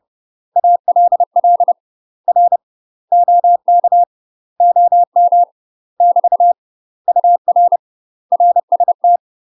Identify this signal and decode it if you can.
Morse code, heard as a keyed tone, about 30 wpm, 700 Hz. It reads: ALL R OK OM <BT> UR RST
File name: UK MASTER QSOs TRAINING / 30WPM / UK MASTER QSOs 0048